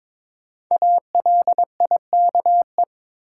Morse code, heard as a keyed tone, about 22 wpm, 700 Hz. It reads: ALIKE
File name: 5 Letter Words - First 3 Letters Given - 22wpm